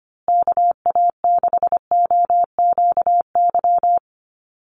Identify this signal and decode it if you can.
Morse code, heard as a keyed tone, about 25 wpm, 700 Hz. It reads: KA6OQY
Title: USA Call Signs Repeated 2x 25wpm